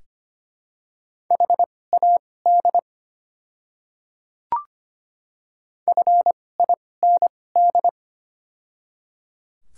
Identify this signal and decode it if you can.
Morse code, heard as a keyed tone, about 25 wpm, spaced as for 18 wpm, 700 Hz. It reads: HAD FIND
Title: 2x Character Spacing - Top 100 Words - 25wpm